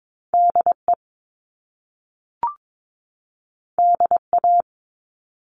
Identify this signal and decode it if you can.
Morse code, heard as a keyed tone, about 22 wpm, 700 Hz. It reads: DE DA
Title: Top Two Letter Combinations in Words 22wpm